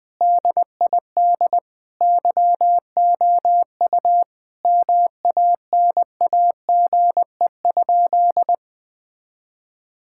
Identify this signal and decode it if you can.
Morse code, heard as a keyed tone, about 21 wpm, 700 Hz. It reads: DID YOU MANAGE?
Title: Sentences from Top 1600 Words 20wpm